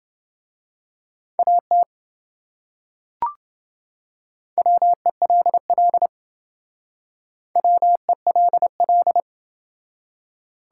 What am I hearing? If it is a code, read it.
Morse code, heard as a keyed tone, about 28 wpm, 700 Hz. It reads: AT WELL WELL